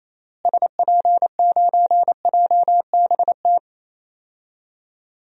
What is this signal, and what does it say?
Morse code, heard as a keyed tone, about 28 wpm, 700 Hz. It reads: SP9JBT
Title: DXCC-100 Call Signs - 6 Characters - Repeated 3x - 28wpm